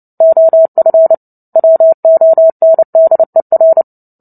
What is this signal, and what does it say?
Morse code, heard as a keyed tone, about 30 wpm, 650 Hz. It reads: OF WONDER
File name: Book Two / 03-Chapter 3 - 29 WPM